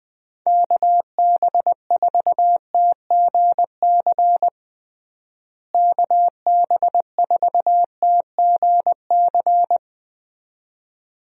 Morse code, heard as a keyed tone, about 20 wpm, 700 Hz. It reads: KB4TGC KB4TGC
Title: USA Call Signs - 6 Characters - Repeated 2x - 20wpm